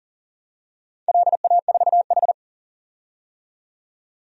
Morse code, heard as a keyed tone, about 40 wpm, 700 Hz. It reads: LA4H
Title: DXCC-100 Call Signs - 4 Characters - 40wpm